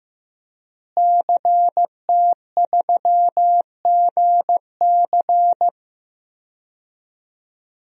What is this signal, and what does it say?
Morse code, heard as a keyed tone, about 16 wpm, 700 Hz. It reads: CT3GC
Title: DXCC-100 Call Signs - 5 Characters - Repeated 3x - 15wpm